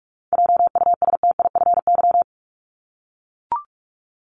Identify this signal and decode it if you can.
Morse code, heard as a keyed tone, about 44 wpm, 700 Hz. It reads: JUSTIFY
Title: Top 1751-2000 Words 45wpm